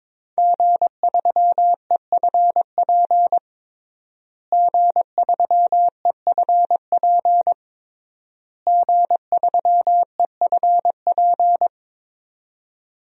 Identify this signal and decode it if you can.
Morse code, heard as a keyed tone, about 22 wpm, 700 Hz. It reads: G3EFP G3EFP G3EFP
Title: DXCC-100 Call Signs - 5 Characters - Repeated 3x - 22wpm